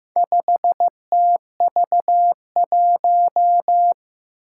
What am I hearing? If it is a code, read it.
Morse code, heard as a keyed tone, about 15 wpm, 700 Hz. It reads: 5TV1
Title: Four Challenging Letter-Number Combinations 15wpm